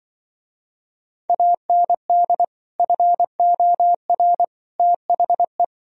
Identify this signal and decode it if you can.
Morse code, heard as a keyed tone, about 24 wpm, 700 Hz. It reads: AND FOR THE
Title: Speed-Racing - Sets of 5 Words 20wpm